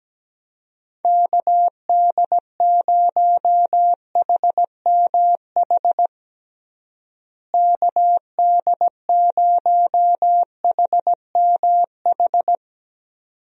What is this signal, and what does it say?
Morse code, heard as a keyed tone, about 17 wpm, 700 Hz. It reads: KD0HMH KD0HMH
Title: USA Call Signs - 6 Characters - Repeated 2x - 17wpm